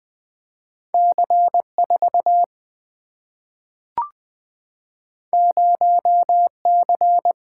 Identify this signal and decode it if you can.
Morse code, heard as a keyed tone, about 20 wpm, 700 Hz. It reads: C4 0C
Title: Single-Letter Single-Number 20wpm